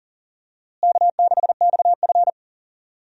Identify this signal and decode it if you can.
Morse code, heard as a keyed tone, about 40 wpm, 700 Hz. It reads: K6XF
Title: USA Call Signs - 4 Characters - Repeated 2x - 40wpm